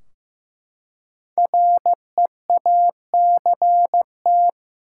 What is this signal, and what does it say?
Morse code, heard as a keyed tone, about 15 wpm, 700 Hz. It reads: REACT
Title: Five Letter Words - 15wpm